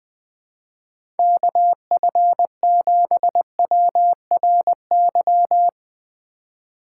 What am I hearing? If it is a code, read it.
Morse code, heard as a keyed tone, about 20 wpm, 700 Hz. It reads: KF7WRY
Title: USA Call Signs - 2x Character Spacing - 20wpm